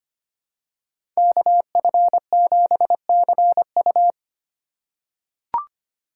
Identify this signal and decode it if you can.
Morse code, heard as a keyed tone, about 25 wpm, 700 Hz. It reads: KF7CU
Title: USA Call Signs - 2x Character Spacing - 25wpm